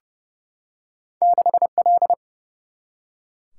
Morse code, heard as a keyed tone, about 30 wpm, 700 Hz. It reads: 6L
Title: Single-Letter Single-Number 30wpm